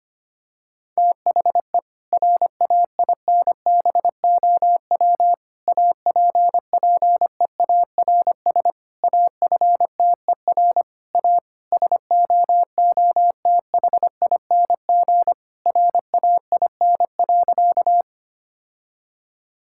Morse code, heard as a keyed tone, about 25 wpm, 700 Hz. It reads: THE RAINBOW APPEARS AFTER A SOOTHING RAIN.
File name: practice / 25wpm / Difficult Sentences with Hints - 25wpm